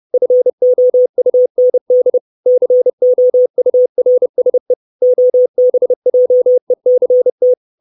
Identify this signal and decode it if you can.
Morse code, heard as a keyed tone, about 30 wpm, 500 Hz. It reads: FOUND COURSE OBJECT
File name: 632 Words 30WPM